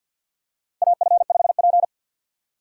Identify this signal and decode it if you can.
Morse code, heard as a keyed tone, about 52 wpm, 700 Hz. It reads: AF5P
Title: K1USN SST QSO Practice - 50wpm